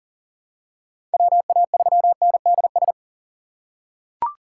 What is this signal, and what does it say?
Morse code, heard as a keyed tone, about 37 wpm, 700 Hz. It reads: WA3NDS
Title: USA Call Signs - 6 Characters - 4x Character Spacing - 40wpm